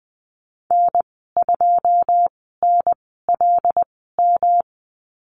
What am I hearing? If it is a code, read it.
Morse code, heard as a keyed tone, about 20 wpm, spaced as for 14 wpm, 700 Hz. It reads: N2NLM
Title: 128 Lesson 21g - Number 6 - Call Signs 20wpm